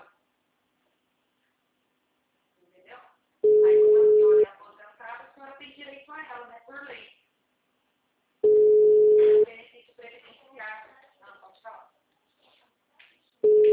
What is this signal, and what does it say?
Ringback tone (Europe, 425 Hz), with speech.
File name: out- -2025- - - .61398